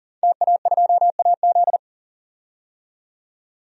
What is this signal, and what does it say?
Morse code, heard as a keyed tone, about 37 wpm, 700 Hz. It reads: TA2AZ
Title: DXCC-100 Call Signs - 5 Characters - 40wpm